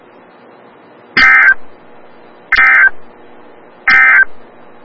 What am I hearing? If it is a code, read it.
Emergency Alert System (SAME) tones, heard as harsh buzzing bursts. It is heard from 1.0 to 4.3 s.
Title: Des Moines, IA NOAA Weather Radio - WXL57